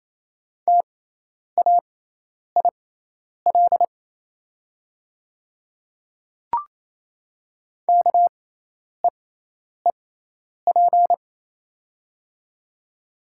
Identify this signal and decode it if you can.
Morse code, heard as a keyed tone, about 28 wpm, spaced as for 10 wpm, 700 Hz. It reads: TAIL KEEP
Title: 6x Character Spacing - Top 500 Words - 28wpm